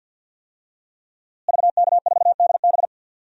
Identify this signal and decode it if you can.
Morse code, heard as a keyed tone, about 48 wpm, 700 Hz. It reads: VK4DB
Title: DXCC-100 Call Signs - 5 Characters - Repeated 3x - 50wpm